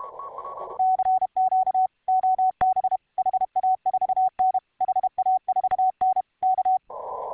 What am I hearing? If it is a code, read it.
Morse code, heard as a keyed tone, about 31 wpm, 750 Hz, with radio noise. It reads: CQ OB HA4N HA4N K